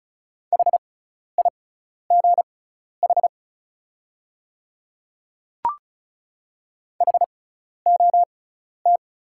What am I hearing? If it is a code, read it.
Morse code, heard as a keyed tone, about 36 wpm, spaced as for 12 wpm, 690 Hz. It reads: HIGH HOT
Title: 6x Character Spacing - Top 300 Words - 35wpm